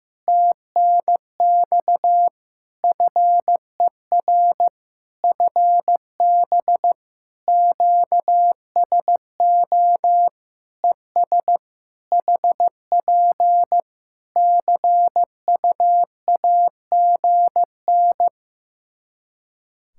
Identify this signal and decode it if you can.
Morse code, heard as a keyed tone, about 15 wpm, 700 Hz. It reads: TNX FER FB QSO ES HP CUAGN
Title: Practice QSO - Exchange 4 - Ending - 15wpm